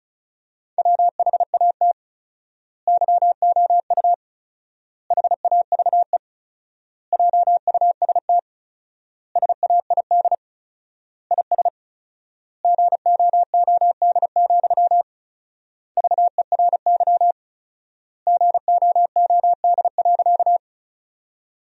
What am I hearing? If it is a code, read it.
Morse code, heard as a keyed tone, about 36 wpm, 700 Hz. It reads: WHAT YOU HAVE JUST SAID IS GOOD, VERY GOOD.
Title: 4x Word-Spacing - Sentences from Top 200 Words - 35wpm